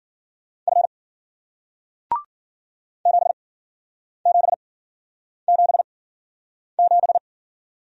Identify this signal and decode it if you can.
Morse code, heard as a keyed tone, about 50 wpm, spaced as for 9 wpm, 700 Hz. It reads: F 7777